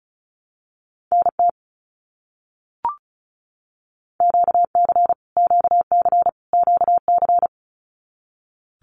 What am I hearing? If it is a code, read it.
Morse code, heard as a keyed tone, about 34 wpm, 700 Hz. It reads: NT QC QC QC
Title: Canadian Provinces and Territories - Repeated 3x - 35wpm